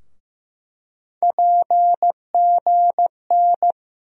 Morse code, heard as a keyed tone, about 15 wpm, 700 Hz. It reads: PGN
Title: USA Call Sign Postfixes - 3 Letters 15wpm